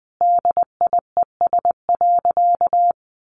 Morse code, heard as a keyed tone, about 20 wpm, 700 Hz. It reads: DIES.
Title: A Sound Of Thunder 20wpm